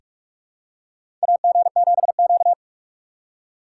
Morse code, heard as a keyed tone, about 46 wpm, 690 Hz. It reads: AG7Q
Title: USA Call Signs - 4 Characters - 2x Character Spacing - 45wpm